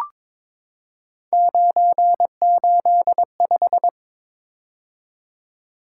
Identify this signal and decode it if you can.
Morse code, heard as a keyed tone, about 22 wpm, 700 Hz. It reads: 985
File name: practice / 22wpm / Three Letter-Number Combinations - 22wpm